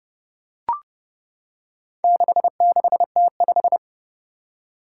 Morse code, heard as a keyed tone, about 32 wpm, 700 Hz. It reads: 66T5
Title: Four Challenging Letter-Number Combinations 30wpm